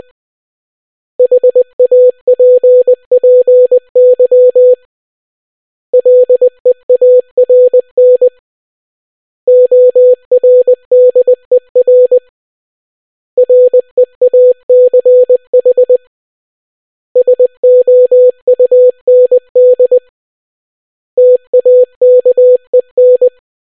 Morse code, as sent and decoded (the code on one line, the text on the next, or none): .... .- .--. .--. -.-- / .-.. .- .-. -. / --- .-. -.. .-. / .-. .- -.-. .... / ... --- ..- -. -.. / - .- -.- . -.
HAPPY LEARN ORDER REACH SOUND TAKEN